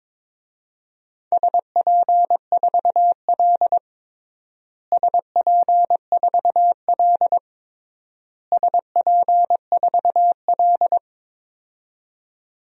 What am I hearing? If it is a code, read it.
Morse code, heard as a keyed tone, about 22 wpm, 700 Hz. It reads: SP4L SP4L SP4L
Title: DXCC-100 Call Signs - 4 Characters - Repeated 3x - 22wpm